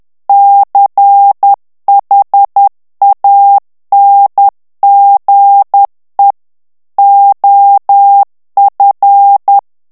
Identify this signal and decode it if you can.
Morse code, heard as a keyed tone, about 10 wpm, 800 Hz. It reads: CHANGE OF
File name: ANGLAIS texte clair 3 vitesse 10 mots minute